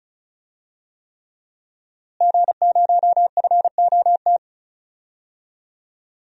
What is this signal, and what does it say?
Morse code, heard as a keyed tone, about 36 wpm, 700 Hz. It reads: G0FOT
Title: DXCC-100 Call Signs - 5 Characters - Repeated 3x - 35wpm